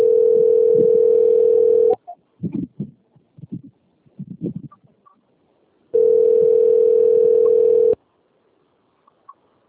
Ringback tone (North America), 440 + 480 Hz.